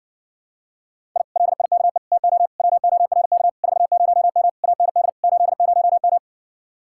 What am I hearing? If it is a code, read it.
Morse code, heard as a keyed tone, about 51 wpm, 700 Hz. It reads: I LIKE TO WORK 40M AND 80M